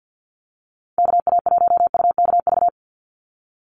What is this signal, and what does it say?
Morse code, heard as a keyed tone, about 50 wpm, 700 Hz. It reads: KA1UKV
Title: USA Call Signs 50wpm